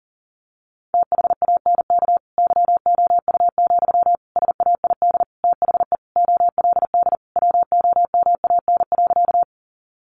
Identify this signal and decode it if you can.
Morse code, heard as a keyed tone, about 40 wpm, 700 Hz. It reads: THANK YOU, SAID THE OLD WOMAN.
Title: Sentences from Top 1100 Words 40wpm